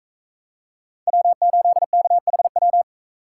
Morse code, heard as a keyed tone, about 42 wpm, 700 Hz. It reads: W8KHW